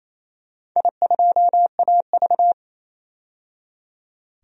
Morse code, heard as a keyed tone, about 28 wpm, 690 Hz. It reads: I2AV
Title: DXCC-100 Call Signs - 4 Characters - 28wpm